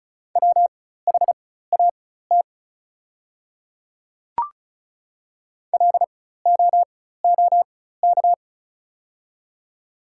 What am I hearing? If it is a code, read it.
Morse code, heard as a keyed tone, about 36 wpm, spaced as for 16 wpm, 700 Hz. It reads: WHAT LOOK
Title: 4x Character Spacing - Top 200 Words - 35wpm